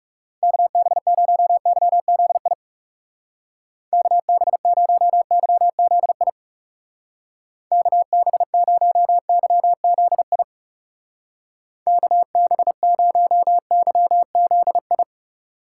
Morse code, heard as a keyed tone, about 37 wpm, 700 Hz. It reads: KB0YZI KB0YZI KB0YZI KB0YZI
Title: Speed Racing - USA 2x3 Call Signs 30wpm